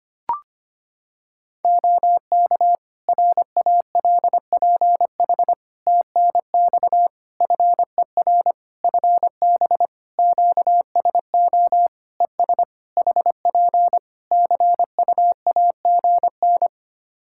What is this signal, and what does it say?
Morse code, heard as a keyed tone, about 25 wpm, 700 Hz. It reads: OK RALPH TNX FER FB QSO ES HP CUAGN
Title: Practice QSO - Exchange 4 - Ending - 25wpm